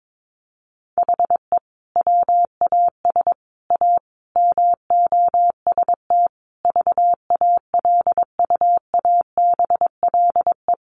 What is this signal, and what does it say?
Morse code, heard as a keyed tone, about 22 wpm, 700 Hz. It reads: HE WAS A MOST VALUABLE
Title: Sentences from Top 1200 Words 22wpm